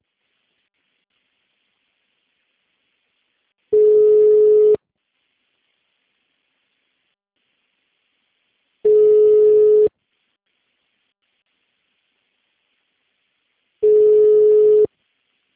Ringback tone (Europe, 425 Hz).
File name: out- -2003- - - .74